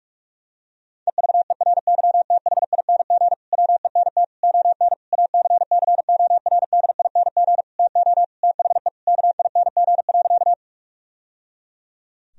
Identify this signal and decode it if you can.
Morse code, heard as a keyed tone, about 46 wpm, 700 Hz. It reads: EVERYTHING WENT ON ACCORDING TO THE KING.
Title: Sentences from Top 1000 Words - Review Part 4 - 45wpm